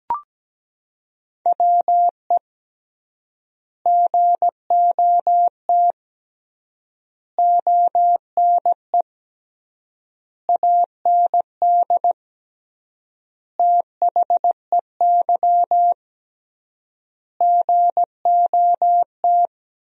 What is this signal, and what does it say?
Morse code, heard as a keyed tone, about 17 wpm, 700 Hz. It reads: WE GOT ONE AND THEY GOT
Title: 3x Word-Spacing - Sentences from Top 300 Words - 17wpm